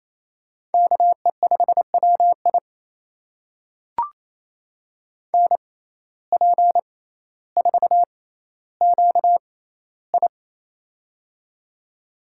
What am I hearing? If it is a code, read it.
Morse code, heard as a keyed tone, about 28 wpm, spaced as for 14 wpm, 700 Hz. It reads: KE5WI NP4QI
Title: USA Call Signs - 5 Characters - 6x Character Spacing - 28wpm